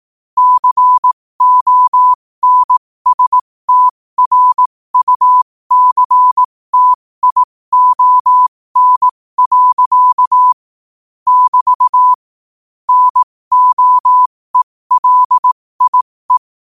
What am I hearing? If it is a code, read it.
Morse code, heard as a keyed tone, about 18 wpm, spaced as for 15 wpm, 1000 Hz. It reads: CONSTRUCTION. <BT> NOELIE
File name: qotdfiles / 15wpm